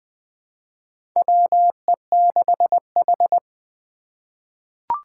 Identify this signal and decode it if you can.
Morse code, heard as a keyed tone, about 20 wpm, 700 Hz. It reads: WE6H
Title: USA Call Signs - 4 Characters - 6x Character Spacing - 20wpm